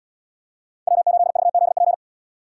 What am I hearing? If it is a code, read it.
Morse code, heard as a keyed tone, about 68 wpm, 700 Hz. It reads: W8VZP